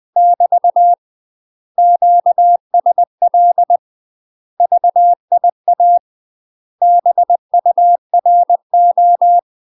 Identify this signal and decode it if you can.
Morse code, heard as a keyed tone, about 20 wpm, 700 Hz. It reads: <BT> QSL VIA BURO
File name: UK MASTER QSOs TRAINING / 20WPM / UK MASTER QSOs 0063